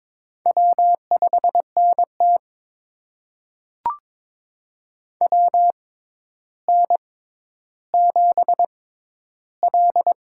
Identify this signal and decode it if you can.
Morse code, heard as a keyed tone, about 22 wpm, spaced as for 11 wpm, 700 Hz. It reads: W5NT WN7L
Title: USA Call Signs - 4 Characters - 6x Character Spacing - 22wpm